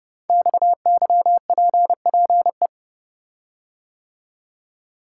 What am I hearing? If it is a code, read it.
Morse code, heard as a keyed tone, about 30 wpm, 700 Hz. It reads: XYPPE
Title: Code Groups - 5 Letters - 30wpm